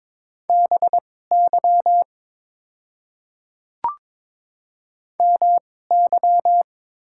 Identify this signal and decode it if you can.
Morse code, heard as a keyed tone, about 22 wpm, spaced as for 16 wpm, 690 Hz. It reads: BY MY
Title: 2x Character Spacing - Top 100 Words - 22wpm